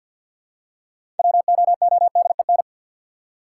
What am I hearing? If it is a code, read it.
Morse code, heard as a keyed tone, about 50 wpm, 700 Hz. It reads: WOODEN
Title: Six Letter Words - 50wpm